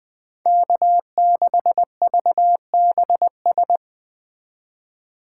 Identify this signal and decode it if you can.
Morse code, heard as a keyed tone, about 20 wpm, 700 Hz. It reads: K6VBS
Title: Speed Racing - USA 1x3 Call Signs 17wpm